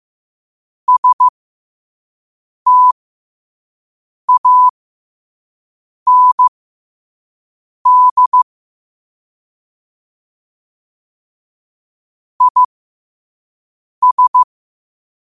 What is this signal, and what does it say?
Morse code, heard as a keyed tone, about 15 wpm, spaced as for 5 wpm, 1000 Hz. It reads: STAND IS
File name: qotdfiles / 5wpm